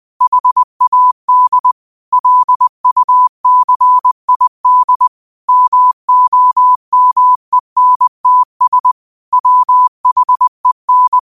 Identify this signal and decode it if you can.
Morse code, heard as a keyed tone, about 20 wpm, 1000 Hz. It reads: HAD LUCID MOMENTS WHEN